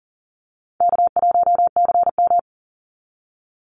Morse code, heard as a keyed tone, about 39 wpm, 700 Hz. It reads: K1CM
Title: NA SOTA Chaser Callsigns 40wpm